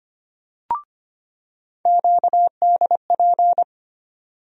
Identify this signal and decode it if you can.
Morse code, heard as a keyed tone, about 25 wpm, 700 Hz. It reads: QDP
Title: USA Call Sign Postfixes - 3 Letters 25wpm